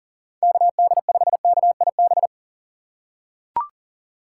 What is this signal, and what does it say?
Morse code, heard as a keyed tone, about 40 wpm, 700 Hz. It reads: KD5KIB
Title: USA Call Signs - 6 Characters - 4x Character Spacing - 40wpm